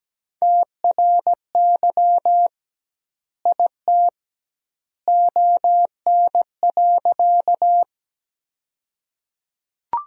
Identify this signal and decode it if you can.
Morse code, heard as a keyed tone, about 17 wpm, 700 Hz. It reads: TRY IT ON.
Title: 2x Word-Spacing - Sentences from Top 200 Words - 17wpm